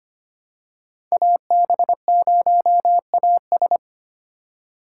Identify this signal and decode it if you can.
Morse code, heard as a keyed tone, about 25 wpm, 700 Hz. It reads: AB0AS
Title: USA Call Signs - 5 Characters - 4x Character Spacing - 25wpm